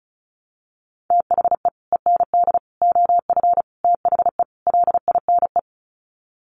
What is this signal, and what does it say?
Morse code, heard as a keyed tone, about 34 wpm, 700 Hz. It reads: THE END OF THE LINE
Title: Sets of 5 Words Encore2 35wpm